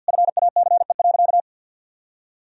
Morse code, heard as a keyed tone, about 50 wpm, 700 Hz. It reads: FACE.
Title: Sentences from Top 1000 Words - Review Part 2 - 50wpm